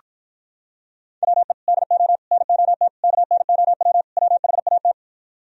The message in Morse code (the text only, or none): WE DO NOT KNOW WHAT